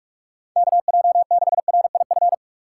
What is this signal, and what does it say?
Morse code, heard as a keyed tone, about 46 wpm, 700 Hz. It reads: KJ6RIF
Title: USA Call Signs - 6 Characters - 4x Character Spacing - 45wpm